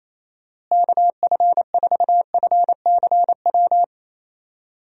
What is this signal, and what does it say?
Morse code, heard as a keyed tone, about 28 wpm, 700 Hz. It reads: KF4FCW
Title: USA Call Signs - 6 Characters - Repeated 3x - 28wpm